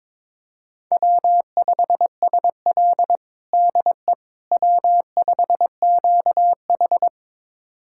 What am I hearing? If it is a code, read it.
Morse code, heard as a keyed tone, about 22 wpm, 700 Hz. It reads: W5SL DE W5QH
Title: USA Callsign DE Callsign - 4 Characters - 22wpm